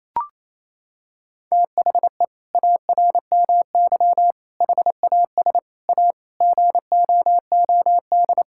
Morse code, heard as a keyed tone, about 28 wpm, 700 Hz. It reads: THE ARMY HAS A GOOD
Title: Sentences from Top 1000 Words - Review Part 5 - 28wpm